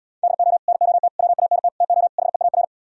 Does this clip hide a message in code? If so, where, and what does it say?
Yes: Morse code from 0.2 to 2.6 s, heard as a keyed tone, about 77 wpm, 700 Hz. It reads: DO NOT WANT TO HEAR